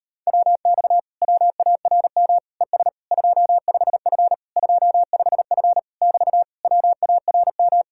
Morse code, heard as a keyed tone, about 39 wpm, 700 Hz. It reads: WX WARM ES 25F 25F <BT> WARM